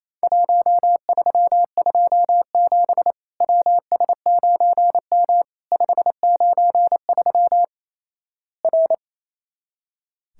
Morse code, heard as a keyed tone, about 28 wpm, 700 Hz. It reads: 1327 WS9M 593 R
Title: Worked All Europe DX Contest - QTC Exchange - 28wpm